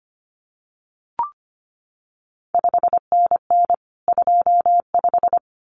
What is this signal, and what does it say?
Morse code, heard as a keyed tone, about 25 wpm, 700 Hz. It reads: 5NN 25
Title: CQ WW DX Contest Exchange - 25wpm